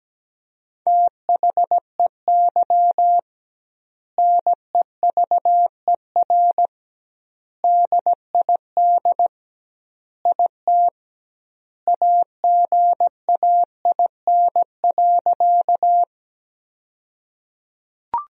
Morse code, heard as a keyed tone, about 17 wpm, 700 Hz. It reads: THEY NEVER DID IT AGAIN.
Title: 2x Word-Spacing - Sentences from Top 300 Words - 17wpm